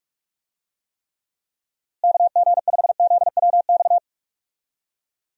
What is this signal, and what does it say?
Morse code, heard as a keyed tone, about 46 wpm, 700 Hz. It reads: KG5ZWX
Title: USA Call Signs - 6 Characters - Repeated 2x - 45wpm